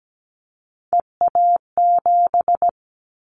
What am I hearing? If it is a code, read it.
Morse code, heard as a keyed tone, about 17 wpm, 700 Hz. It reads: EA7
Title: Callsign Prefixes for Top 25 DXCC - 17wpm